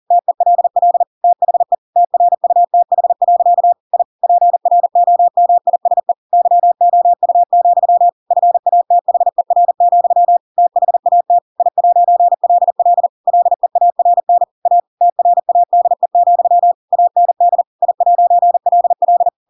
Morse code, heard as a keyed tone, about 40 wpm, 700 Hz. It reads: TELL THE TRUTH. I PROMISE YOU, FATHER, THAT I'LL LEARN A TRADE, AND I'LL